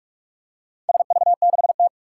Morse code, heard as a keyed tone, about 44 wpm, 700 Hz. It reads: SV6T
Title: Four Challenging Letter-Number Combinations 45wpm